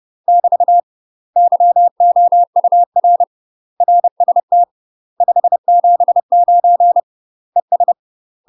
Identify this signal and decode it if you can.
Morse code, heard as a keyed tone, about 30 wpm, 700 Hz. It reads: <BT> YOUR RST 579 ES